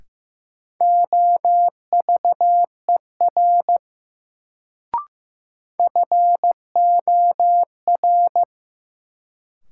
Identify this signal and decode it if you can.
Morse code, heard as a keyed tone, about 15 wpm, 700 Hz. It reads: OVER FOR
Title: Sets of 2 Words - New Word Drill - 15wpm